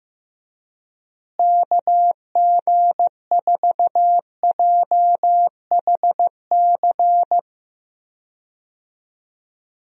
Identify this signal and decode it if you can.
Morse code, heard as a keyed tone, about 15 wpm, 700 Hz. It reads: KG4JHC
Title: USA Call Signs - 6 Characters - Repeated 2x - 15wpm